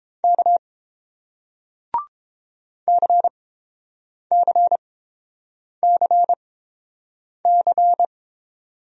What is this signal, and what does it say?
Morse code, heard as a keyed tone, about 30 wpm, spaced as for 7 wpm, 700 Hz. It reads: K CCCC